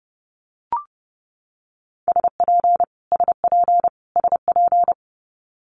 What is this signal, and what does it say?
Morse code, heard as a keyed tone, about 30 wpm, 700 Hz. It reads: SP SP SP
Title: Multi-letter Phonemes Repeated 3x 30wpm